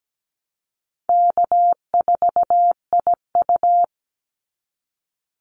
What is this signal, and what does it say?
Morse code, heard as a keyed tone, about 17 wpm, 700 Hz. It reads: K4IU
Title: Call Signs - 17wpm